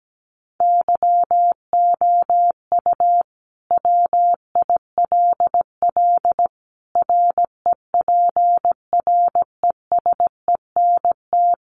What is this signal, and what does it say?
Morse code, heard as a keyed tone, about 17 wpm, 700 Hz. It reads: YOU WILL REPRESENT
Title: Sentences from Top 1200 Words 17wpm